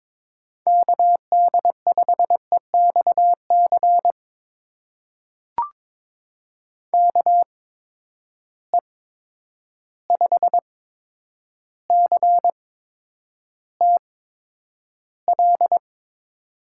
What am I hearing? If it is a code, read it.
Morse code, heard as a keyed tone, about 22 wpm, spaced as for 9 wpm, 700 Hz. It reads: KD5EXC KE5CTL
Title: USA Call Signs - 6 Characters - 8x Character Spacing - 22wpm